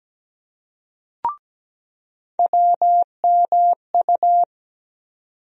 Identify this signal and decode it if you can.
Morse code, heard as a keyed tone, about 17 wpm, 700 Hz. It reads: WMU
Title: USA Call Sign Postfixes - 3 Letters 17wpm